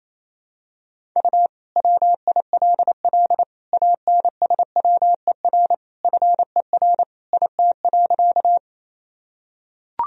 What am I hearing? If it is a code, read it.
Morse code, heard as a keyed tone, about 28 wpm, 700 Hz. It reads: U WILL ANSWER FER IT.
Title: Sentences from Top 300 Words and Abbreviations 28wpm